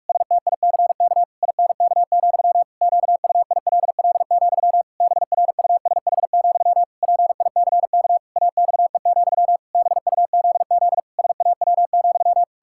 Morse code, heard as a keyed tone, about 46 wpm, 700 Hz. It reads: STICK INK, QUILL, BRUSH, PICK AXE, BUZZ SAW,
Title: Pangram Challenge v2 - 45wpm